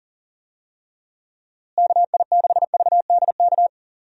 Morse code, heard as a keyed tone, about 40 wpm, 700 Hz. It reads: KI6VDK
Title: USA Call Signs - 6 Characters - Repeated 2x - 40wpm